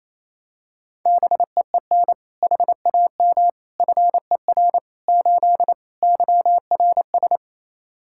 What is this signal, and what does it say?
Morse code, heard as a keyed tone, about 28 wpm, 700 Hz. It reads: BEEN HAM FER 8 YRS
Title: Practice QSO - Exchange 3 - More Info - 28wpm